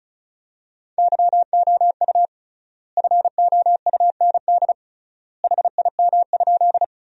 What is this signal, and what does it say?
Morse code, heard as a keyed tone, about 36 wpm, 700 Hz. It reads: YOU FOUND HIM?
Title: 3x Word-Spacing - Sentences from Top 300 Words - 35wpm